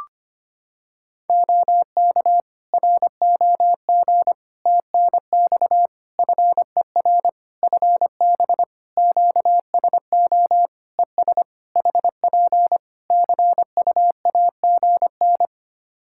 Morse code, heard as a keyed tone, about 25 wpm, 700 Hz. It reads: OK ROG TNX FER FB QSO ES HP CUAGN
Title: Practice QSO - Exchange 4 - Ending - 25wpm